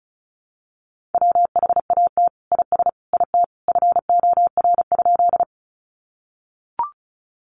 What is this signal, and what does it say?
Morse code, heard as a keyed tone, about 34 wpm, 700 Hz. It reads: WHAT IS IT FOR?
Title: Sentences from Top 100 Words 35wpm